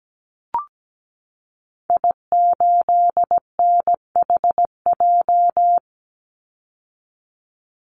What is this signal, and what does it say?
Morse code, heard as a keyed tone, about 17 wpm, 700 Hz. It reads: I8NHJ
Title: Call Signs - 17wpm